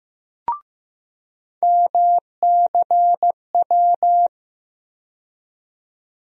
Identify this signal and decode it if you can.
Morse code, heard as a keyed tone, about 15 wpm, 700 Hz. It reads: MCW
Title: USA Call Sign Postfixes - 3 Letters 15wpm